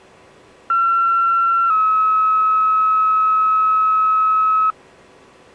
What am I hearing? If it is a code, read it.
Two-tone page, heard as a one-second tone, then a three-second tone, about 1360 then 1290 Hz. It is heard from 0.7 to 4.7 s.